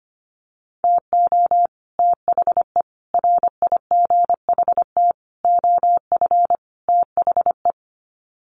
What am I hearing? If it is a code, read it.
Morse code, heard as a keyed tone, about 25 wpm, 700 Hz. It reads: TO THE RIGHT OF THE
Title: Sets of 5 Words Encore 25wpm